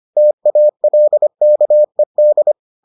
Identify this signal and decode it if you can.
Morse code, heard as a keyed tone, about 25 wpm, 600 Hz. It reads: TALKED